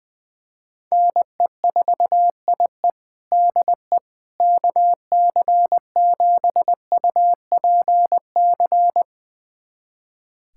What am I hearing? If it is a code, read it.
Morse code, heard as a keyed tone, about 21 wpm, 700 Hz. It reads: NE4IE DE KC7UPC
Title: USA Callsign DE Callsign 20wpm